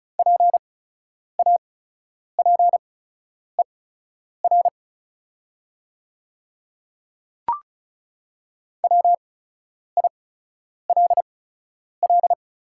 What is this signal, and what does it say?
Morse code, heard as a keyed tone, about 36 wpm, spaced as for 10 wpm, 700 Hz. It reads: PAPER WILL
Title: 8x Character Spacing - Top 500 Words - 35wpm